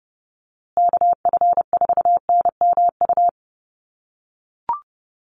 Morse code, heard as a keyed tone, about 30 wpm, 700 Hz. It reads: KF4NMU
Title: USA Call Signs Repeated 3x 30wpm